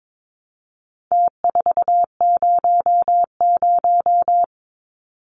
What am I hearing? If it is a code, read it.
Morse code, heard as a keyed tone, about 22 wpm, 700 Hz. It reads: T400
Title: Numbers - Contest Style - 4-Digit 22wpm